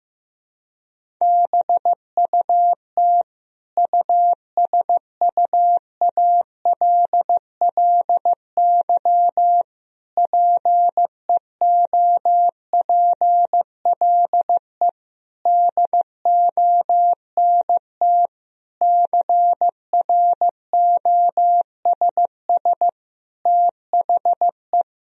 Morse code, heard as a keyed tone, about 16 wpm, 700 Hz. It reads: BUT USUALLY PEOPLE DONT CROSS THE
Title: Sentences from Top 1000 Words - Review Part 3 - 15wpm